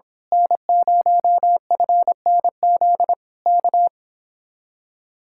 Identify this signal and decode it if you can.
Morse code, heard as a keyed tone, about 26 wpm, 700 Hz. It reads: N0FNZ K
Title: Speed-Racing Common QSO Phrases - 22wpm